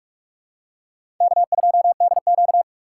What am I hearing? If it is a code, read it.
Morse code, heard as a keyed tone, about 46 wpm, 700 Hz. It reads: K2DQ
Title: USA Call Signs - 4 Characters - 6x Character Spacing - 45wpm